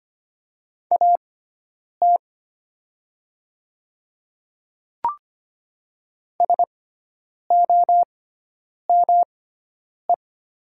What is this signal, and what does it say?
Morse code, heard as a keyed tone, about 25 wpm, spaced as for 9 wpm, 700 Hz. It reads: AT SOME
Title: 6x Character Spacing - Top 100 Words - 25wpm